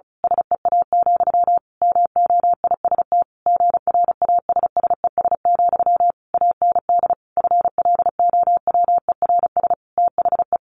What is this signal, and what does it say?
Morse code, heard as a keyed tone, about 34 wpm, 700 Hz. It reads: SEA, MOIST GRASSES, AND FLOWERS THE
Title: A Sound Of Thunder 35wpm